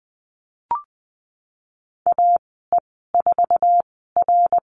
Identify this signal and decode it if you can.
Morse code, heard as a keyed tone, about 20 wpm, spaced as for 14 wpm, 700 Hz. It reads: AE4R
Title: 069 Lesson 12g - Letter d - Call Signs 20wpm